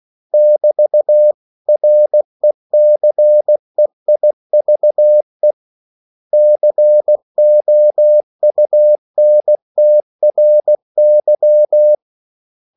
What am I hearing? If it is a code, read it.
Morse code, heard as a keyed tone, about 16 wpm, 600 Hz. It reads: <BT>RECEIVE COUNTRY
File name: CW / Practice / 500MostCommonEnglishWords16